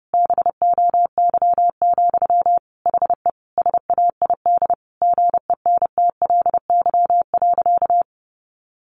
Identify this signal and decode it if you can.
Morse code, heard as a keyed tone, about 30 wpm, 700 Hz. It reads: BOY, HE SAID GENTLY.
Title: Sentences from Top 1500 Words 30wpm